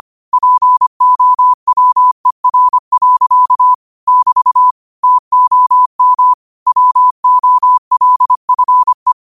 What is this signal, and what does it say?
Morse code, heard as a keyed tone, about 25 wpm, 1000 Hz. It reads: POWER. <BT> TOM WOLFE
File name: 25wpm